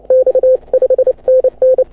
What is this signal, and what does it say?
Morse code, heard as a keyed tone, about 36 wpm, 530 Hz. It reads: X 5 NN
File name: AUDIO2018 / HA6OI CW